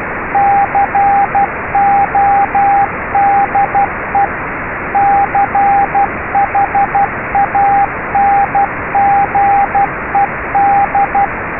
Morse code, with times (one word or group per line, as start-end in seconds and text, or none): 0.3-4.2 CODE
5.0-11.2 CHANGED